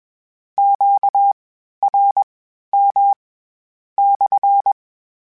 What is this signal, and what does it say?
Morse code, heard as a keyed tone, about 21 wpm, spaced as for 12 wpm, 800 Hz. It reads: QRM /
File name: Q Calls / Q Calls-mp3s / Q CALLS SPSL @20